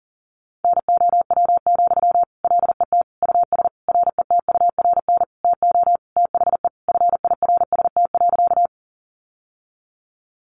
Morse code, heard as a keyed tone, about 40 wpm, 700 Hz. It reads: NOW, LET US RETURN TO THE FIRST.
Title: Sentences from Top 800 Words 40wpm